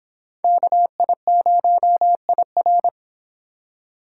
Morse code, heard as a keyed tone, about 26 wpm, 700 Hz. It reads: KI0IR